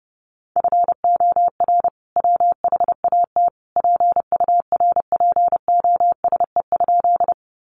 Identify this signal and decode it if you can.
Morse code, heard as a keyed tone, about 30 wpm, 700 Hz. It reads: FOR WHAT PURPOSE?
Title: Sentences from Top 800 Words 30wpm